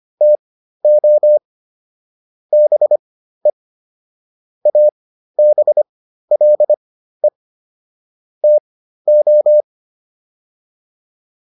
Morse code, heard as a keyed tone, about 25 wpm, spaced as for 13 wpm, 600 Hz. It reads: TO BE ABLE TO